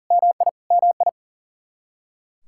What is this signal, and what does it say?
Morse code, heard as a keyed tone, about 40 wpm, 700 Hz. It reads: MI MI
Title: Practice QSO - Exchange 1 - Beginning - 40wpm